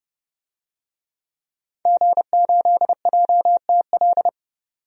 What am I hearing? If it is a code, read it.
Morse code, heard as a keyed tone, about 30 wpm, 700 Hz. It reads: G8JTL
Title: DXCC-100 Call Signs - 5 Characters - 30wpm